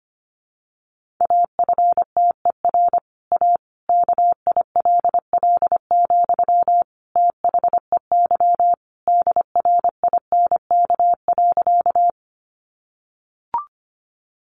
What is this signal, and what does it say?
Morse code, heard as a keyed tone, about 25 wpm, 700 Hz. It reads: AFTER A KILL, THEY DRINK.
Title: Sentences from Top 1500 Words 25wpm